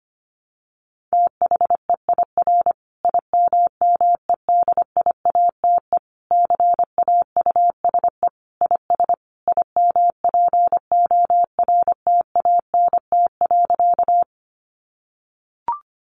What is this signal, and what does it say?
Morse code, heard as a keyed tone, about 25 wpm, 700 Hz. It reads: THEIR IMMEDIATE CAUSE IS IMPORTANT.
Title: Sentences from Top 1200 Words 25wpm